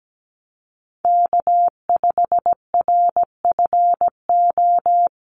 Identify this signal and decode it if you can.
Morse code, heard as a keyed tone, about 17 wpm, 700 Hz. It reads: K5RFO
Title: USA Call Signs Repeated 2x 17wpm